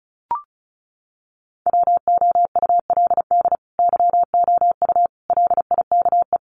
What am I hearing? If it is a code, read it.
Morse code, heard as a keyed tone, about 34 wpm, 700 Hz. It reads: WOULD YOU LIKE
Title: Sentences from Top 100 Words 35wpm